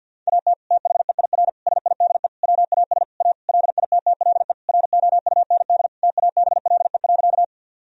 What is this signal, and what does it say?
Morse code, heard as a keyed tone, about 50 wpm, 700 Hz. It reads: AT THEIR SIDE WAS A LITTLE ROUND TABLE.
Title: Sentences from Top 1000 Words - Review Part 2 - 50wpm